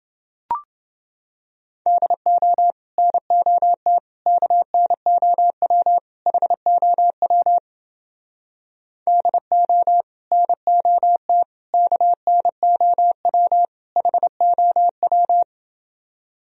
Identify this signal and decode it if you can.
Morse code, heard as a keyed tone, about 29 wpm, 700 Hz. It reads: DO NOT KNOW HOW DO NOT KNOW HOW